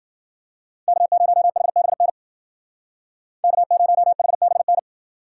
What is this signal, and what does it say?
Morse code, heard as a keyed tone, about 50 wpm, 700 Hz. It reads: K0HBN K0HBN